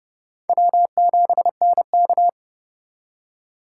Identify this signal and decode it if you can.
Morse code, heard as a keyed tone, about 31 wpm, 700 Hz. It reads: W7NK